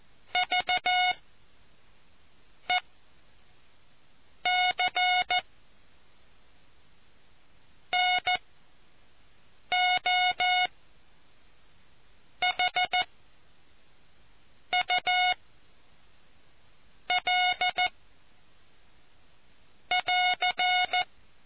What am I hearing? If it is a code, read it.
Morse code, heard as a keyed tone, about 14 wpm, 730 Hz. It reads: V E C N O H U L <AR>